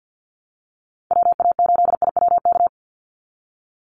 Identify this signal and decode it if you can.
Morse code, heard as a keyed tone, about 50 wpm, 700 Hz. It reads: WA8IWK
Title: Call Signs - 50wpm